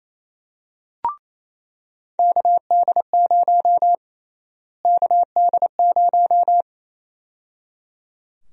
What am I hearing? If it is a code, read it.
Morse code, heard as a keyed tone, about 28 wpm, 700 Hz. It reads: KD0 KD0
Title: USA Call Sign Prefixes - Repeated 2x - 28wpm